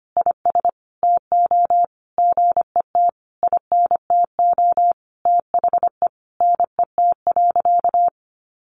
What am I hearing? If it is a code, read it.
Morse code, heard as a keyed tone, about 25 wpm, 700 Hz. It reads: IS TO GET INTO THE NET.
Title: Sentences from Top 1400 Words 25wpm